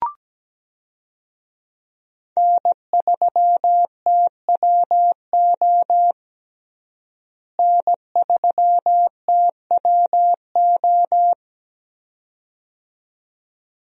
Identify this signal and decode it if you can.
Morse code, heard as a keyed tone, about 17 wpm, 700 Hz. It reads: N3TWO N3TWO
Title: USA Call Signs - 5 Characters - Repeated 2x - 17wpm